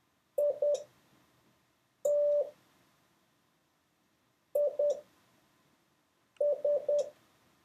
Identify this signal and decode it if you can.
Morse code, heard as a keyed tone, about 10 wpm, spaced as for 5 wpm, 580 Hz. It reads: IT IS
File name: 6of6-test